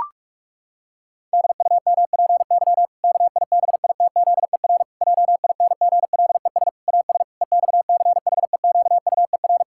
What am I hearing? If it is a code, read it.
Morse code, heard as a keyed tone, about 48 wpm, 700 Hz. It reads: DUMPY KIBITZER JINGLES AS EXCHEQUER
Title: Pangram Challenge v2 - 45wpm